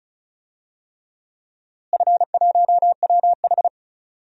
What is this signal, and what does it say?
Morse code, heard as a keyed tone, about 36 wpm, 700 Hz. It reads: F1WH
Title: DXCC-100 Call Signs - 4 Characters - Repeated 3x - 35wpm